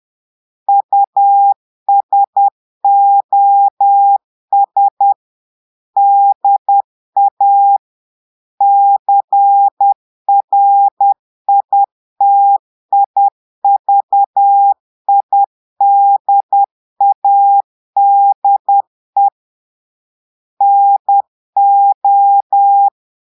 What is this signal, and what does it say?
Morse code, heard as a keyed tone, about 10 wpm, 800 Hz. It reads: USOS DA CRITIVIDADE NO
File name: Portugais - texte en clair 1 - vitesse - 10 mots minute